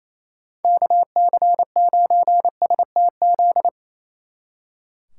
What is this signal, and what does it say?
Morse code, heard as a keyed tone, about 33 wpm, 700 Hz. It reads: KC9STZ
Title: Speed Racing - USA 2x3 Call Signs 28wpm